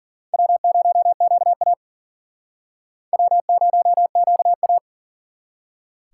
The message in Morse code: W0QA W0QA

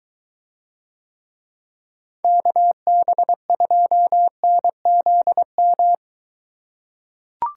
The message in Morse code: KB2NZM

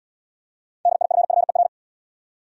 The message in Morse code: DIFFER

